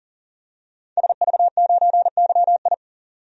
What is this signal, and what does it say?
Morse code, heard as a keyed tone, about 40 wpm, 700 Hz. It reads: SV9YI
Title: DXCC-100 Call Signs - 5 Characters - 40wpm